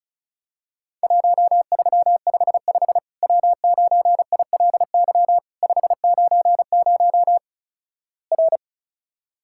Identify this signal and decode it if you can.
Morse code, heard as a keyed tone, about 36 wpm, 700 Hz. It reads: 1355 W9ILY 590 R
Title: Worked All Europe DX Contest - QTC Exchange - 35wpm